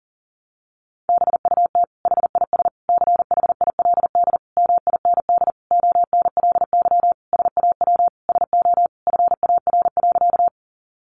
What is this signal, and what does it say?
Morse code, heard as a keyed tone, about 40 wpm, 700 Hz. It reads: BUT HIS CHILD MIND ONLY SAW SO FAR.
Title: Sentences from Top 800 Words 40wpm